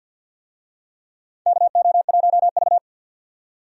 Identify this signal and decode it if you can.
Morse code, heard as a keyed tone, about 51 wpm, 700 Hz. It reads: KO1V
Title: USA Call Signs - 4 Characters - 50wpm